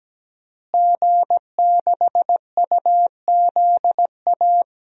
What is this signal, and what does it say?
Morse code, heard as a keyed tone, about 17 wpm, 700 Hz. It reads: G6UZA
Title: DXCC-100 Call Signs - 5 Characters - Repeated 2x - 17wpm